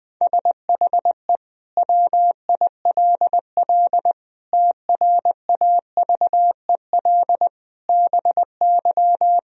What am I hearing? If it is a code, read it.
Morse code, heard as a keyed tone, about 20 wpm, 700 Hz. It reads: SHE WILL TRAVEL BY
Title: Sentences from Top 1000 Words - Review Part 3 - 20wpm